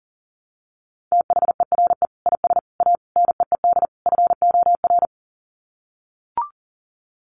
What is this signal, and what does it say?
Morse code, heard as a keyed tone, about 40 wpm, 700 Hz. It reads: THERE IS A NEED FOR
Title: Sets of 5 Words Encore2 40wpm